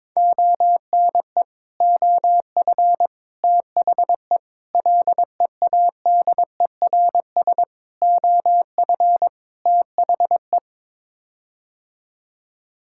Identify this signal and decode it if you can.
Morse code, heard as a keyed tone, about 22 wpm, 700 Hz. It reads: ONE OF THE LEADERS OF THE
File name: practice / 22wpm / Sets of 6 Words - 22wpm